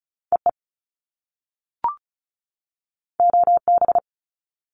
Morse code, heard as a keyed tone, about 36 wpm, 700 Hz. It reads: EE OB
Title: Top Two Letter Combinations in Words 35wpm